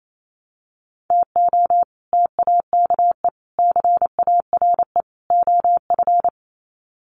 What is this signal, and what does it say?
Morse code, heard as a keyed tone, about 28 wpm, 700 Hz. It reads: TO TAKE CARE OF
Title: Sets of 4 Words 28wpm